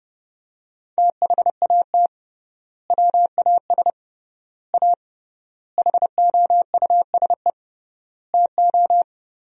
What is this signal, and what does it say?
Morse code, heard as a keyed tone, about 30 wpm, 700 Hz. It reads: THAT WAS A HOUSE TO
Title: 3x Word-Spacing - Sentences from Top 200 Words - 30wpm